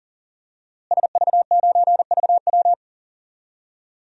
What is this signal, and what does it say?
Morse code, heard as a keyed tone, about 37 wpm, 700 Hz. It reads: SV9VW
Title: DXCC-100 Call Signs - 5 Characters - Repeated 3x - 40wpm